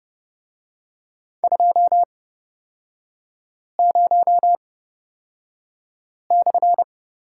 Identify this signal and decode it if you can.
Morse code, heard as a keyed tone, about 31 wpm, spaced as for 5 wpm, 700 Hz. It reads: 20/
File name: Lesson 36e - BT - Three Characters 30wpm